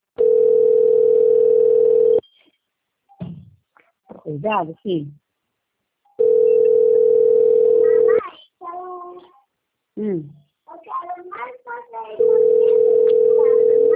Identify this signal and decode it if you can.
Ringback tone (North America), 440 + 480 Hz, with speech.
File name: rg-1600-unknown- - - .6308